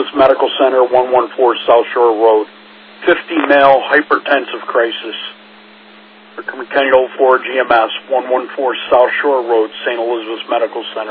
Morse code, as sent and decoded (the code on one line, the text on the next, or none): none